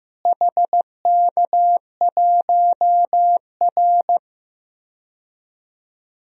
Morse code, heard as a keyed tone, about 15 wpm, 700 Hz. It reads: HK1R
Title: DXCC-100 Call Signs - 4 Characters - Repeated 3x - 15wpm